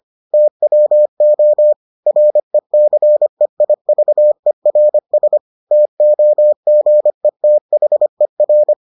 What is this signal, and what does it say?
Morse code, heard as a keyed tone, about 25 wpm, 600 Hz. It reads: TWO RECEIVERS TOGETHER